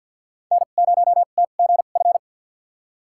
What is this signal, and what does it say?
Morse code, heard as a keyed tone, about 46 wpm, spaced as for 35 wpm, 700 Hz. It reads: N0TGF
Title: USA Call Signs - 5 Characters - 2x Character Spacing - 50wpm